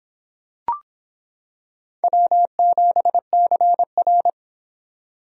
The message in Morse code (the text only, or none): W7CR